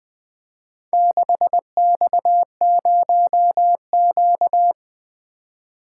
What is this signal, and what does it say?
Morse code, heard as a keyed tone, about 20 wpm, 700 Hz. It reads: 6X0Q
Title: Four Letter-Number Combinations - 20wpm